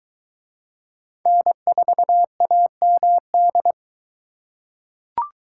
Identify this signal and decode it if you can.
Morse code, heard as a keyed tone, about 20 wpm, 700 Hz. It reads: N4AMD